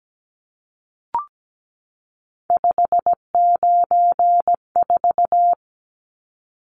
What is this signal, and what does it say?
Morse code, heard as a keyed tone, about 17 wpm, 700 Hz. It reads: 594
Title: Numbers - 3-Digit 17wpm